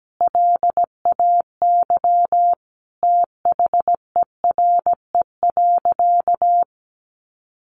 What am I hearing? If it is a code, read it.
Morse code, heard as a keyed tone, about 17 wpm, 700 Hz. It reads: LAY THERE.
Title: Sentences from Top 1500 Words 17wpm